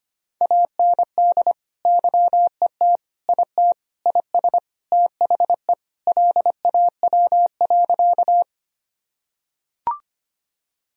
Morse code, heard as a keyed tone, about 25 wpm, 700 Hz. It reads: AND YET IT IS THE LAW.
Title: Sentences from Top 1000 Words - Review Part 3 - 25wpm